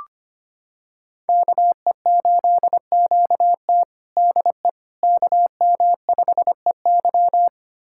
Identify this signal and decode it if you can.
Morse code, heard as a keyed tone, about 25 wpm, 700 Hz. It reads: KE8QT DE KM5EY
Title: USA Callsign DE Callsign - 5 Characters - 25wpm